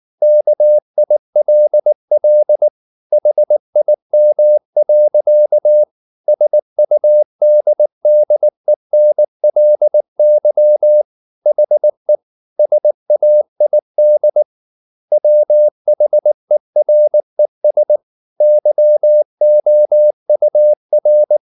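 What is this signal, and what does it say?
Morse code, heard as a keyed tone, about 19 wpm, 600 Hz. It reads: KILL HIM. SUDDENLY HE SAID WHERES YOUR
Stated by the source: Morse code at 19 wpm from CW med Gnister, reading Ken Follett's novel Never in English.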